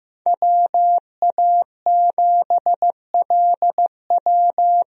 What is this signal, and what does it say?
Morse code, heard as a keyed tone, about 15 wpm, 700 Hz. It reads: WA7LW
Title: USA Call Signs - 5 Characters - Repeated 3x - 15wpm